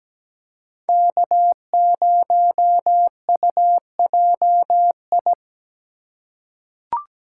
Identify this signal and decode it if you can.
Morse code, heard as a keyed tone, about 17 wpm, 700 Hz. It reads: K0UJI
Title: USA Call Signs - 5 Characters - 4x Character Spacing - 17wpm